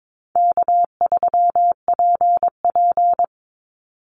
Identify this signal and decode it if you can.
Morse code, heard as a keyed tone, about 22 wpm, 700 Hz. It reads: K3PP
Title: Easy Call Signs Repeated 3x - 22wpm